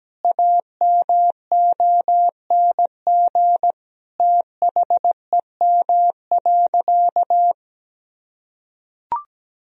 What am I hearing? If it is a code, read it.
Morse code, heard as a keyed tone, about 17 wpm, 700 Hz. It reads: AMONG THEM.
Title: Sentences from Top 1000 Words - Review Part 3 - 17wpm